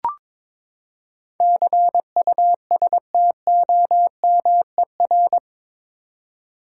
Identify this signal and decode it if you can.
Morse code, heard as a keyed tone, about 22 wpm, 700 Hz. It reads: CUSTOMER
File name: practice / 22wpm / Eight Letter Words - 22wpm